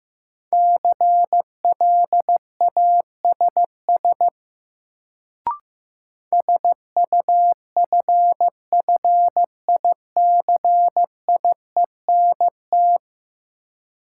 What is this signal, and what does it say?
Morse code, heard as a keyed tone, about 15 wpm, 700 Hz. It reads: CLASS SUFFICIENT
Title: Sentences from Top 1400 Words - New Word Drill Alt - 15wpm